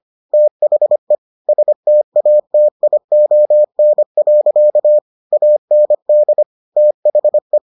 Morse code, heard as a keyed tone, about 25 wpm, 600 Hz. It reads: THE STATION. AND THE